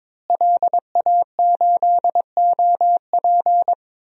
Morse code, heard as a keyed tone, about 22 wpm, 700 Hz. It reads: LA8OP
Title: DXCC-100 Call Signs - 5 Characters - Repeated 3x - 22wpm